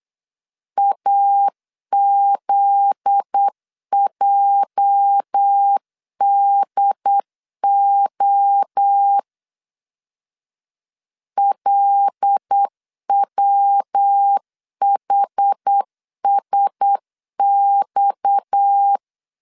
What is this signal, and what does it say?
Morse code, heard as a keyed tone, about 8 wpm, 800 Hz. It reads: AZJDO LWHSX
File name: LEZIONI / 8 YJ45 / 8 YJ 8wpm 3313